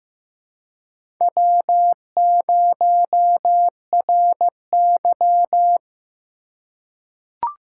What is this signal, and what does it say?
Morse code, heard as a keyed tone, about 15 wpm, 700 Hz. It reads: W0RY
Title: USA Call Signs - 3x Character Spacing - 15wpm